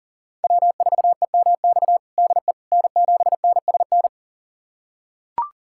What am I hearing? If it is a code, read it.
Morse code, heard as a keyed tone, about 37 wpm, 690 Hz. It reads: W4EMX DE N7NSN
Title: USA Callsign DE Callsign - 5 Characters - 40wpm